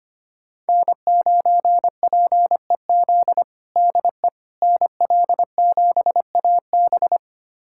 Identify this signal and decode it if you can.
Morse code, heard as a keyed tone, about 25 wpm, 700 Hz. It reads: N9PEZ DE NL7AB
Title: USA Callsign DE Callsign - 5 Characters - 25wpm